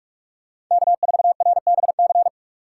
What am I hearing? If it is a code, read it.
Morse code, heard as a keyed tone, about 45 wpm, 700 Hz. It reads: K4RBC